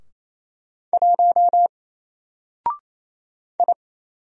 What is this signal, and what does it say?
Morse code, heard as a keyed tone, about 27 wpm, 700 Hz. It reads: 1 I
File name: Single-Letter-Number-28wpm